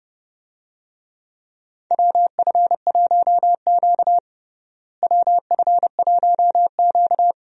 Morse code, heard as a keyed tone, about 30 wpm, 700 Hz. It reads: WF1Q WF1Q
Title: USA Call Signs - 4 Characters - Repeated 3x - 30wpm